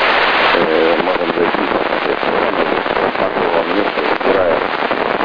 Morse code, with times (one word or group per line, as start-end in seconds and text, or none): none